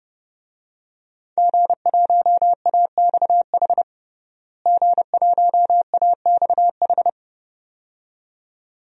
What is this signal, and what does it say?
Morse code, heard as a keyed tone, about 31 wpm, 700 Hz. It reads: G1AXH G1AXH
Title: DXCC-100 Call Signs - 5 Characters - Repeated 2x - 30wpm